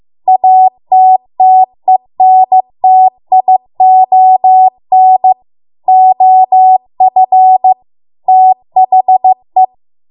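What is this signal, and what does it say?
Morse code, heard as a keyed tone, about 15 wpm, 750 Hz. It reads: ATTENTION OF THE
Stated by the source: Morse code practice at 15 wpm from the national amateur radio association's code archive.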